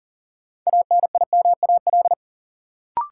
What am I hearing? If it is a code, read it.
Morse code, heard as a keyed tone, about 40 wpm, 700 Hz. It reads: ANIMAL
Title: Sentences from Top 1000 Words - Review Part 1 - New Word Drill - 40wpm